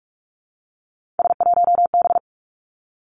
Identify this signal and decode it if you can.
Morse code, heard as a keyed tone, about 46 wpm, 700 Hz. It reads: S1B
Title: Three Challenging Letter-Number Combinations 45wpm